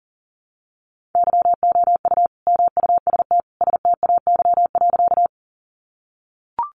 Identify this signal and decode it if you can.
Morse code, heard as a keyed tone, about 40 wpm, 700 Hz. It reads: YOU MUST STAY.
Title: Sentences from Top 500 Words 40wpm